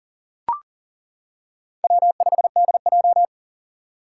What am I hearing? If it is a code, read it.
Morse code, heard as a keyed tone, about 40 wpm, 700 Hz. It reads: W5DJ
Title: Four Challenging Letter-Number Combinations 40wpm